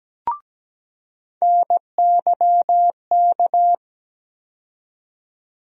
Morse code, heard as a keyed tone, about 17 wpm, 700 Hz. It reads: NYK
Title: USA Call Sign Postfixes - 3 Letters 17wpm